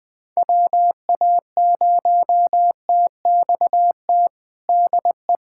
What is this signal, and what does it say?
Morse code, heard as a keyed tone, about 20 wpm, 700 Hz. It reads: WA0TXT DE
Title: USA Callsign DE Callsign 20wpm